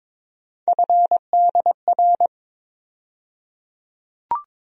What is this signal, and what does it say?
Morse code, heard as a keyed tone, about 22 wpm, 700 Hz. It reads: FDR
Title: USA Call Sign Postfixes - 3 Letters 22wpm